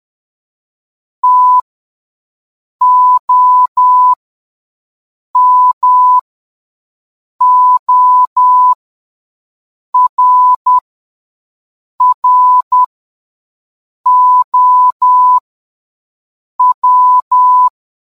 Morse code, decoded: TOMORROW